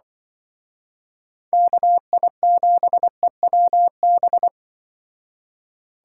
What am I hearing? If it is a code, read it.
Morse code, heard as a keyed tone, about 25 wpm, 700 Hz. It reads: KI7EWB
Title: Speed Racing - USA 2x3 Call Signs 20wpm